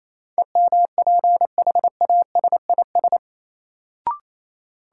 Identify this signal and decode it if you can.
Morse code, heard as a keyed tone, about 28 wpm, 700 Hz. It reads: EMPHASIS
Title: Sentences from Top 1400 Words - New Word Drill - 28wpm